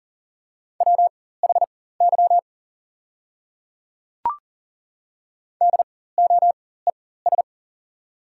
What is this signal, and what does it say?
Morse code, heard as a keyed tone, about 40 wpm, spaced as for 19 wpm, 690 Hz. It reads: WHY DOES
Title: 4x Character Spacing - Top 500 Words - 40wpm